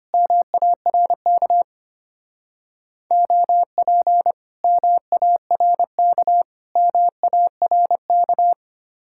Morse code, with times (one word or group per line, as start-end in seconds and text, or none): 0.1-1.6 MARK
3.1-4.3 OP
4.6-6.4 MARK
6.8-8.5 MARK